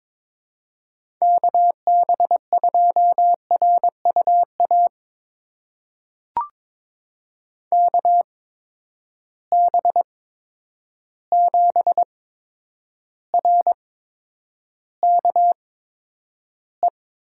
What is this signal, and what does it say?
Morse code, heard as a keyed tone, about 22 wpm, spaced as for 9 wpm, 700 Hz. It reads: KB2RUA KB7RKE
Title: USA Call Signs - 6 Characters - 8x Character Spacing - 22wpm